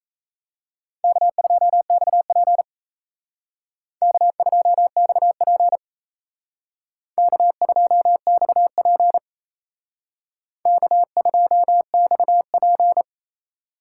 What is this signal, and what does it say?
Morse code, heard as a keyed tone, about 32 wpm, 700 Hz. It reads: K2XP K2XP K2XP K2XP
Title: Speed Racing - USA Call Signs 28wpm